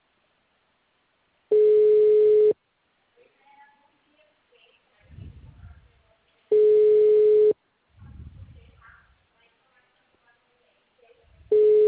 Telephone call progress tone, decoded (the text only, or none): ringback tone